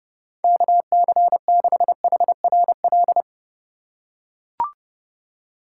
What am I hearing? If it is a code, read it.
Morse code, heard as a keyed tone, about 30 wpm, 700 Hz. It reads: KC6HRL
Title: USA Call Signs - 6 Characters - 2x Character Spacing - 30wpm